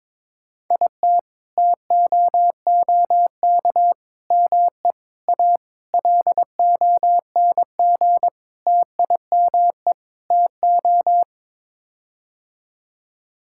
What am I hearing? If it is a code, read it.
Morse code, heard as a keyed tone, about 22 wpm, 700 Hz. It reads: IT TOOK ME A LONG TIME TO
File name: practice / 22wpm / Sets of 7 Words - 22wpm